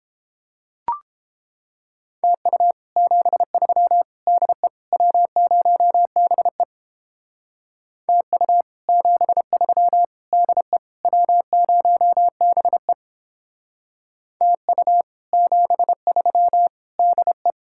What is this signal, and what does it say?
Morse code, heard as a keyed tone, about 31 wpm, 700 Hz. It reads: TU 73 DE W0BE TU 73 DE W0BE TU 73 DE